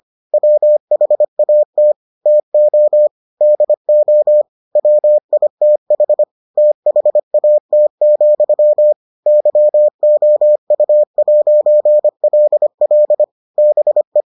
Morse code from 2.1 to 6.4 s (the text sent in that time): TO DO WITH